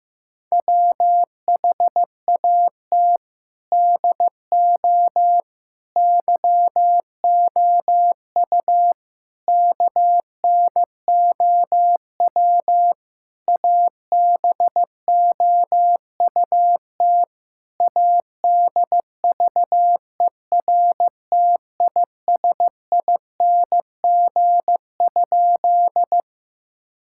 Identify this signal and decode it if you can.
Morse code, heard as a keyed tone, about 15 wpm, 700 Hz. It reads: WHAT DO YOU KNOW ABOUT ADVERTISING?
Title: Sentences from Top 1000 Words - Review Part 6 - 15wpm